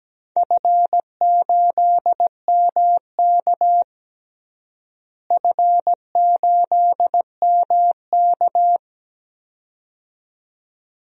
Morse code, heard as a keyed tone, about 17 wpm, 700 Hz. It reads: F8MK F8MK
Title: DXCC-100 Call Signs - 4 Characters - Repeated 2x - 17wpm